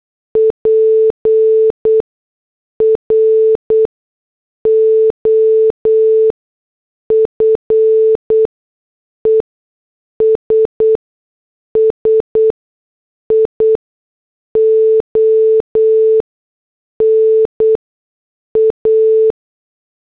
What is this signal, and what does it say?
Morse code, heard as a keyed tone, about 8 wpm, spaced as for 6 wpm, 440 Hz. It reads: PROFESSIONA